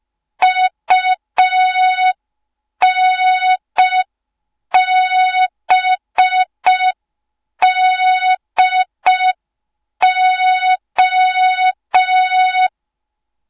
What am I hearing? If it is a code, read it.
Morse code, heard as a keyed tone, about 5 wpm, 3700 Hz. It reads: UNBDO